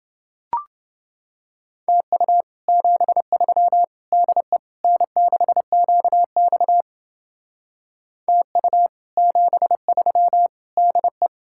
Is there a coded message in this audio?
Yes: Morse code from 1.9 to 11.3 s, heard as a keyed tone, about 29 wpm, 700 Hz. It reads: TU 73 DE N6QX TU 73 DE